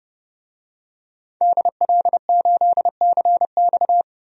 Morse code, heard as a keyed tone, about 29 wpm, 700 Hz. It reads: DL8CX
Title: DXCC-100 Call Signs - 5 Characters - 30wpm